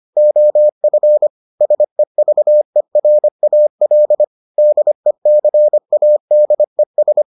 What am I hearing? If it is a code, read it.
Morse code, heard as a keyed tone, about 25 wpm, 600 Hz. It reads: OF SEVERAL DECADES